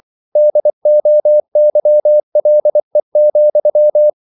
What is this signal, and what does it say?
Morse code, heard as a keyed tone, about 24 wpm, 600 Hz. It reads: DOYLE,
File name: CW / Practice / EnglishQuotations0224